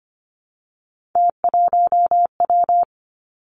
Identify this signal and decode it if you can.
Morse code, heard as a keyed tone, about 25 wpm, 700 Hz. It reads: T1W
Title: Three Challenging Letter-Number Combinations 25wpm